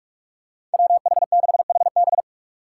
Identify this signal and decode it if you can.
Morse code, heard as a keyed tone, about 46 wpm, 700 Hz. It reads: WH6HB
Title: USA Call Signs - 5 Characters - 3x Character Spacing - 45wpm